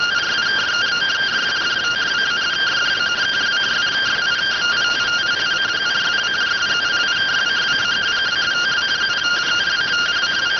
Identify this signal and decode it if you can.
Two-tone teleprinter signal, about 170 Hz shift.